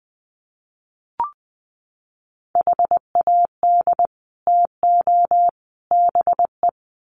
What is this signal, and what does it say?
Morse code, heard as a keyed tone, about 20 wpm, 700 Hz. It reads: HAD TO BE
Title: Sets of 3 Words 20wpm